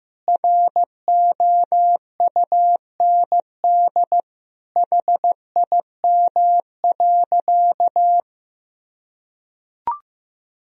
Morse code, heard as a keyed tone, about 16 wpm, 700 Hz. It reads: ROUND HIM.
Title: Sentences from Top 1000 Words - Review Part 1 - 15wpm